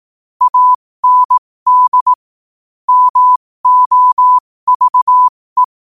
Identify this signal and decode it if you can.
Morse code, heard as a keyed tone, about 18 wpm, spaced as for 15 wpm, 1000 Hz. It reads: AND MOVE